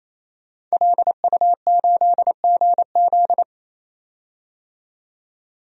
Morse code, heard as a keyed tone, about 27 wpm, 700 Hz. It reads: LU8GZ
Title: DXCC-100 Call Signs - 5 Characters - 28wpm